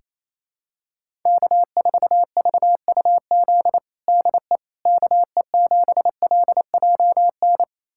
Morse code, heard as a keyed tone, about 28 wpm, 700 Hz. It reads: K4VUZ DE KE7LJN
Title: USA Callsign DE Callsign 28wpm